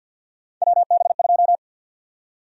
Morse code, heard as a keyed tone, about 49 wpm, 700 Hz. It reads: WB2
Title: USA Call Sign Prefixes - 50wpm